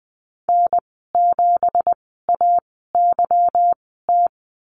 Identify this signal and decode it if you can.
Morse code, heard as a keyed tone, about 20 wpm, spaced as for 15 wpm, 700 Hz. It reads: N7AYT
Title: 179 Lesson 29g - Slash - Call Signs 20wpm